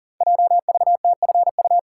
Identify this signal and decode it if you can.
Morse code, heard as a keyed tone, about 40 wpm, 700 Hz. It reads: JVTFU
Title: Code Groups - 5 Letter:Numbers - 40wpm